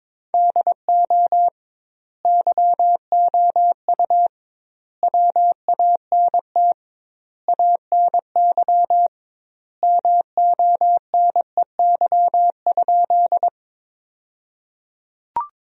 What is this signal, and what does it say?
Morse code, heard as a keyed tone, about 22 wpm, 700 Hz. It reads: DO YOU WANT ANY MONEY?
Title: 2x Word-Spacing - Sentences from Top 400 Words - 22wpm